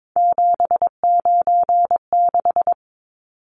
Morse code, heard as a keyed tone, about 22 wpm, 700 Hz. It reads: 796
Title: Numbers - 3-Digit 22wpm